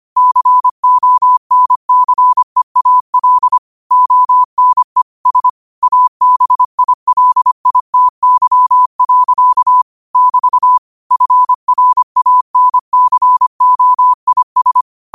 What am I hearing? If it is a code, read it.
Morse code, heard as a keyed tone, about 25 wpm, 1000 Hz. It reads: CONCEAL ONE S ABILITY. <BT> FRANCOIS
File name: qotdfiles / 25wpm